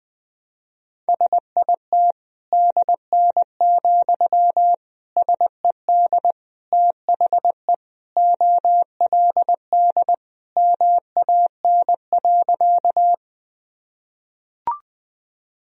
Morse code, heard as a keyed tone, about 20 wpm, 700 Hz. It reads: SIT DN, SED THE OLD MAN.
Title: Sentences from Top 500 Words and Abbreviations 20wpm